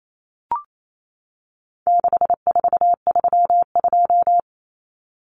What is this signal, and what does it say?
Morse code, heard as a keyed tone, about 28 wpm, 700 Hz. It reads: 6432
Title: Common Cut Numbers - 4-Digit 28wpm